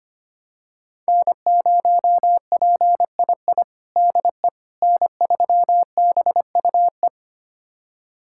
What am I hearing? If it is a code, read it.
Morse code, heard as a keyed tone, about 25 wpm, 700 Hz. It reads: N0PII DE N3BUE
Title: USA Callsign DE Callsign - 5 Characters - 25wpm